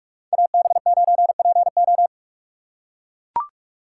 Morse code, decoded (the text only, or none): AB9PO